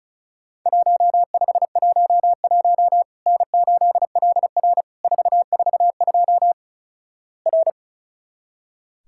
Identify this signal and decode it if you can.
Morse code, heard as a keyed tone, about 36 wpm, 700 Hz. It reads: 1511 N8LR 442 R
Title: Worked All Europe DX Contest - QTC Exchange - 35wpm